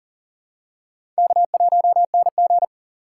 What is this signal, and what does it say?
Morse code, heard as a keyed tone, about 37 wpm, 700 Hz. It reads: K1NG
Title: USA Call Signs - 4 Characters - 4x Character Spacing - 40wpm